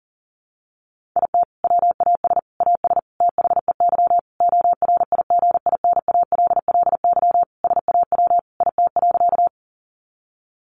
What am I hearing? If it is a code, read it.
Morse code, heard as a keyed tone, about 40 wpm, 700 Hz. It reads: IT WAS AS THEY ORIGINALLY SAW IT.
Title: Sentences from Top 1500 Words 40wpm